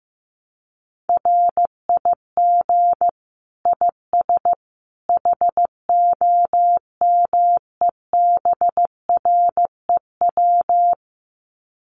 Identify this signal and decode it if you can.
Morse code, heard as a keyed tone, about 15 wpm, 700 Hz. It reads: RIG IS HOMEBREW
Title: Common QSO Phrases 15wpm